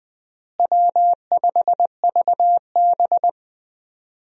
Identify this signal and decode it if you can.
Morse code, heard as a keyed tone, about 21 wpm, 700 Hz. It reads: W5VB
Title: USA Call Signs - 4 Characters - Repeated 2x - 20wpm